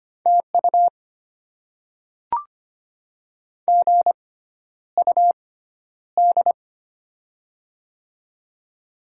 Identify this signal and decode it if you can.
Morse code, heard as a keyed tone, about 25 wpm, spaced as for 11 wpm, 700 Hz. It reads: TU GUD
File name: Top 20 QSO Elements - 6x Character Spacing - 25wpm